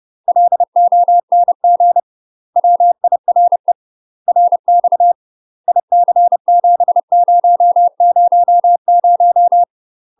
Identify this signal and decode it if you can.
Morse code, heard as a keyed tone, about 30 wpm, 700 Hz. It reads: LONG WIRE RX IC7000
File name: UK MASTER QSOs TRAINING / 30WPM / UK MASTER QSOs 0044